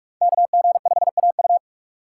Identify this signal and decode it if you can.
Morse code, heard as a keyed tone, about 42 wpm, 700 Hz. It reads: KG5AU
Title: USA Call Signs - 5 Characters - 2x Character Spacing - 45wpm